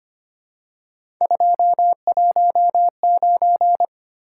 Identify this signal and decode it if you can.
Morse code, heard as a keyed tone, about 25 wpm, 700 Hz. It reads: 219
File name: Three Letter-Number Combinations - 25wpm